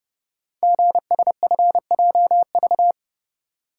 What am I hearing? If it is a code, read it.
Morse code, heard as a keyed tone, about 30 wpm, 700 Hz. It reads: GSFJV
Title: Code Groups - 5 Letters - 30wpm